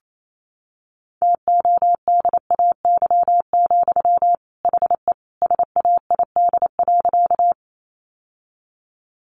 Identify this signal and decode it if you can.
Morse code, heard as a keyed tone, about 28 wpm, 700 Hz. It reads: TODAY, HE SAID.
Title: Sentences from Top 600 Words 28wpm